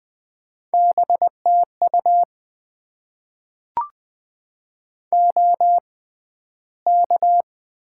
Morse code, heard as a keyed tone, about 20 wpm, spaced as for 12 wpm, 700 Hz. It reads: BTU OK
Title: Top 90 QSO Elements - 6x Character Spacing - 20wpm